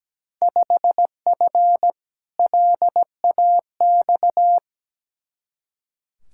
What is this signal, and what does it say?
Morse code, heard as a keyed tone, about 17 wpm, 700 Hz. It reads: 5F LAX
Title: Field Day Exchanges Encore - 17wpm